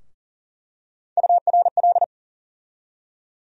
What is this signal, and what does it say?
Morse code, heard as a keyed tone, about 40 wpm, 700 Hz. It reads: URL
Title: USA Call Sign Postfixes - 3 Letters 40wpm